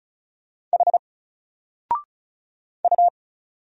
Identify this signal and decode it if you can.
Morse code, heard as a keyed tone, about 33 wpm, 700 Hz. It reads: H U
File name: Single-Letter-Number-35wpm